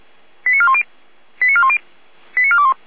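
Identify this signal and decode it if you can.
Five-tone selective call (ZVEI), heard as quick runs of five tones, tones of about 70 ms. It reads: 89410 89410 89421
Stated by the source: ZVEI five-tone selective call 89391 received on SDR, recorded by feuerwehr-orlamuende (fire brigade Orlamünde).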